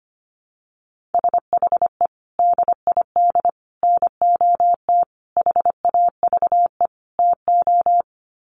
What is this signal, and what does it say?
Morse code, heard as a keyed tone, about 25 wpm, 700 Hz. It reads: SHE DID NOT HAVE TO
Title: Sets of 5 Words Encore2 25wpm